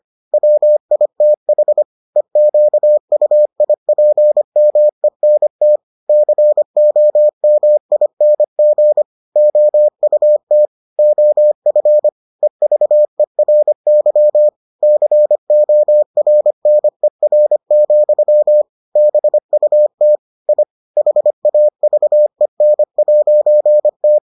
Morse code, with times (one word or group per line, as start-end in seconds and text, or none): 0.3-1.8 WITH
2.2-5.8 EQUIPMENT
6.1-9.0 COMING
9.4-10.7 OUT
11.0-12.1 OF
12.4-14.5 EVERY
14.8-18.6 CORNER,
19.0-20.2 BUT
20.5-20.6 I
21.0-24.2 HAVEN'T